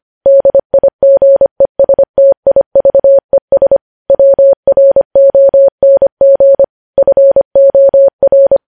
Morse code, heard as a keyed tone, about 25 wpm, 550 Hz. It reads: DIGESTIVES WRONG FOR
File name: podcast / News headlines-1x-25x25